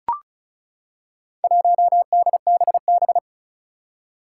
Morse code, heard as a keyed tone, about 36 wpm, 700 Hz. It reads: 1DBB
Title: Four Challenging Letter-Number Combinations 35wpm